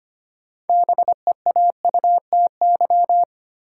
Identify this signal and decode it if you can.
Morse code, heard as a keyed tone, about 25 wpm, 700 Hz. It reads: BEAUTY
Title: Sentences from Top 600 Words - New Word Drill - 25wpm